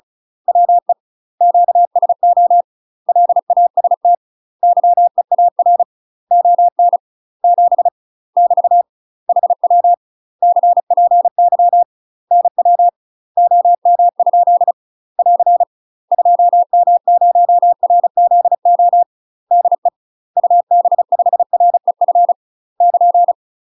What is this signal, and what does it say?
Morse code, heard as a keyed tone, about 34 wpm, 700 Hz. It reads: WE QSO LAST YEAR ON 7 <BT> HW CPY NW OM? <AR> 2M0RZO DE UB5REF <KN>
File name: UK MASTER QSOs TRAINING / 35WPM / UK MASTER QSOs 0024